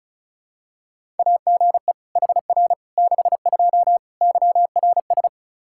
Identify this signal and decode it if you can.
Morse code, heard as a keyed tone, about 36 wpm, 700 Hz. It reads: AGE HR 62 YRS
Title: Practice QSO - Exchange 3 - More Info - 35wpm